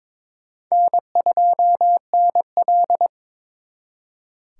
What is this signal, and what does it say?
Morse code, heard as a keyed tone, about 26 wpm, 700 Hz. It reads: N2NL
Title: Speed Racing - USA 1x2 Call Signs 22wpm